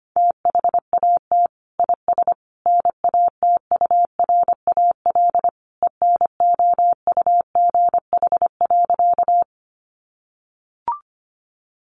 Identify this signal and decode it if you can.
Morse code, heard as a keyed tone, about 25 wpm, 700 Hz. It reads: THAT IS NATURAL ENOUGH.
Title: Sentences from Top 700 Words 25wpm